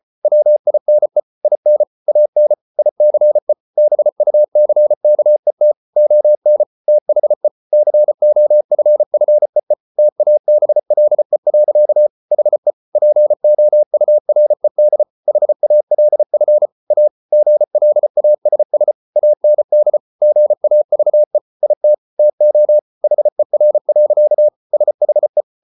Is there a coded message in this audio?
Yes: Morse code from 0.2 to 25.4 s, heard as a keyed tone, about 34 wpm, 600 Hz. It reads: WINE IN AN ICE BUCKET ON THE COFFEE TABLE. HE POURED HALF A GLASS AND GAVE IT TO HER. SHE